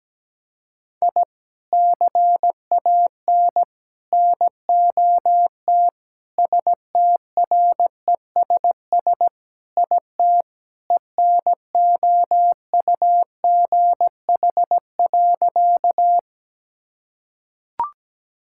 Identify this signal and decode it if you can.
Morse code, heard as a keyed tone, about 17 wpm, 700 Hz. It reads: I CAN NOT STRESS IT ENOUGH.
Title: Sentences from Top 1000 Words - Review Part 6 - 17wpm